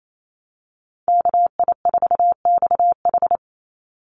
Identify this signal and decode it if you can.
Morse code, heard as a keyed tone, about 28 wpm, 700 Hz. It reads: KI4XH
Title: Call Signs Repeated 3x - 28wpm